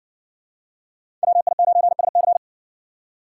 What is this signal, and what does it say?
Morse code, heard as a keyed tone, about 60 wpm, 700 Hz. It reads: WI9SC